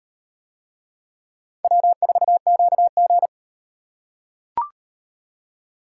Morse code, heard as a keyed tone, about 38 wpm, 700 Hz. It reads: W4QG